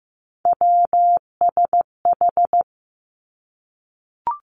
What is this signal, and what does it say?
Morse code, heard as a keyed tone, about 15 wpm, 700 Hz. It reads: WSH
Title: Three Challenging Letter-Number Combinations 15wpm